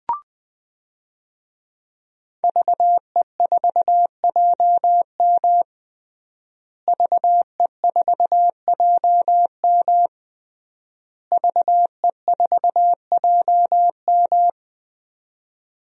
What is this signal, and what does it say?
Morse code, heard as a keyed tone, about 20 wpm, 700 Hz. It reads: VE4JM VE4JM VE4JM
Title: DXCC-100 Call Signs - 5 Characters - Repeated 3x - 20wpm